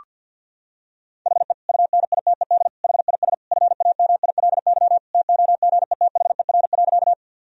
Morse code, heard as a keyed tone, about 50 wpm, 700 Hz. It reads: HE UNITED HIS FAMILY TOGETHER.
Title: Sentences from Top 1000 Words - Review Part 4 - 50wpm